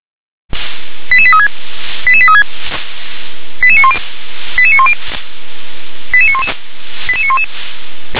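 Five-tone selective call (ZVEI), heard as quick runs of five tones, tones of about 70 ms. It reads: 88836 88836 88910 88910 88911 88911
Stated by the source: ZVEI five-tone selective call received on an SDR.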